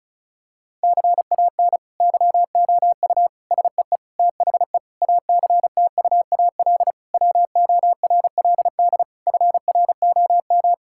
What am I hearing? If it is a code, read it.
Morse code, heard as a keyed tone, about 34 wpm, 700 Hz. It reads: CAN YOU SEE THE ACTUAL WORLD FROM
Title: Sentences from Top 1000 Words - Review Part 5 - 35wpm